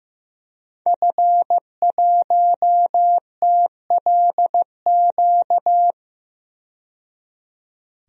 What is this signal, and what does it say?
Morse code, heard as a keyed tone, about 15 wpm, 700 Hz. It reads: F1TLQ
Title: DXCC-100 Call Signs - 5 Characters - 15wpm